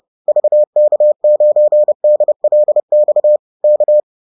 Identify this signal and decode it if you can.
Morse code, heard as a keyed tone, about 30 wpm, 600 Hz. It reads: VK9DLX K